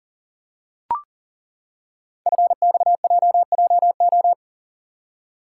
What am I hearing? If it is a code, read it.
Morse code, heard as a keyed tone, about 40 wpm, 690 Hz. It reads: FXJJO
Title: Code Groups - 5 Letters - 40wpm